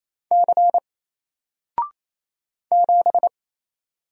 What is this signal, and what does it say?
Morse code, heard as a keyed tone, about 28 wpm, 700 Hz. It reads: C 7
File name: Single-Letter-Number-28wpm